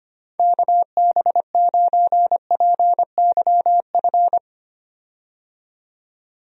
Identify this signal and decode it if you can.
Morse code, heard as a keyed tone, about 25 wpm, 700 Hz. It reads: KB9PYF
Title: USA Call Signs - 6 Characters - 25wpm